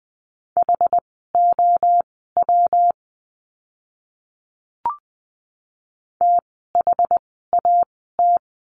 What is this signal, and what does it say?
Morse code, heard as a keyed tone, about 20 wpm, spaced as for 14 wpm, 700 Hz. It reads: HOW THAT
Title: 230 Lesson 37f - Letter x - Words 20wpm